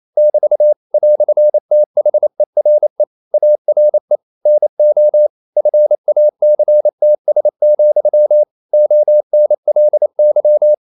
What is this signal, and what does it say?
Morse code, heard as a keyed tone, about 28 wpm, 600 Hz. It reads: <BT> "THERE ARE NO FACTS, ONLY
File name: CW / Practice / EnglishQuotations0128